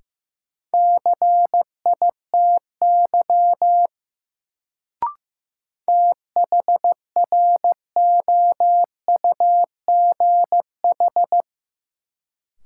Morse code, heard as a keyed tone, about 15 wpm, 700 Hz. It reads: CITY THROUGH
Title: Sets of 2 Words - New Word Drill - 15wpm